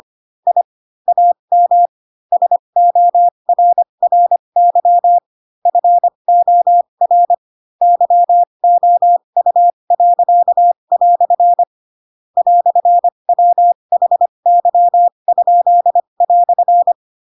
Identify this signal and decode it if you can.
Morse code, heard as a keyed tone, about 25 wpm, spaced as for 22 wpm, 700 Hz. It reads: I AM SORRY FOR YOU." "WHY?"
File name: Pinocchio / Pinocchio ch4 18 25 ik1hge